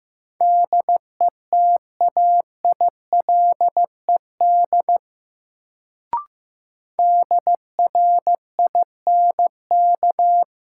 Morse code, heard as a keyed tone, about 15 wpm, 700 Hz. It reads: DETAILED DRINK
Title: Sentences from Top 1500 Words - New Word Drill Alt - 15wpm